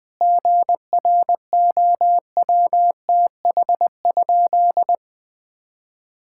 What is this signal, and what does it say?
Morse code, heard as a keyed tone, about 20 wpm, 700 Hz. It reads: GROWTH?
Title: Sentences from Top 1000 Words - Review Part 5 - 20wpm